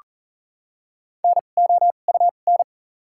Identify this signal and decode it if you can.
Morse code, heard as a keyed tone, about 38 wpm, spaced as for 28 wpm, 700 Hz. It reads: NOUN
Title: 2x Character Spacing - Top 500 Words - 40wpm